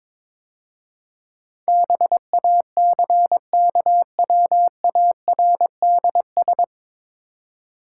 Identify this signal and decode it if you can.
Morse code, heard as a keyed tone, about 21 wpm, 700 Hz. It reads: BACKWARDS
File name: 9 Letter Words - First 3 Letters Given - 22wpm